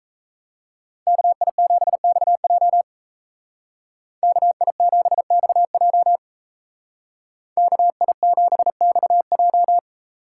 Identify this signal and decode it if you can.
Morse code, heard as a keyed tone, about 33 wpm, 700 Hz. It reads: KI7XJ KI7XJ KI7XJ
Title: Speed Racing - USA 2x2 Call Signs 28wpm